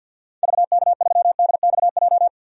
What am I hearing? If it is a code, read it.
Morse code, heard as a keyed tone, about 52 wpm, 700 Hz. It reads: VK3DXJ
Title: DXCC-100 Call Signs - 6 Characters - 50wpm